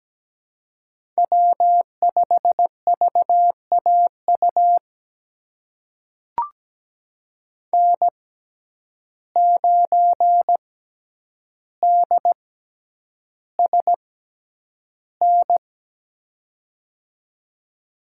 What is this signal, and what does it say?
Morse code, heard as a keyed tone, about 17 wpm, spaced as for 9 wpm, 700 Hz. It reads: W5VAU N9DSN
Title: USA Call Signs - 5 Characters - 6x Character Spacing - 17wpm